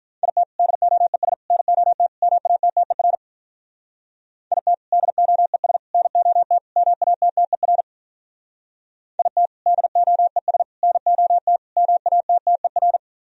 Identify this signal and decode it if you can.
Morse code, heard as a keyed tone, about 43 wpm, 700 Hz. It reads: IT DOES NOT MATTER IT DOES NOT MATTER IT DOES NOT MATTER
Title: Speed-Racing - Sets of 4 Words 35wpm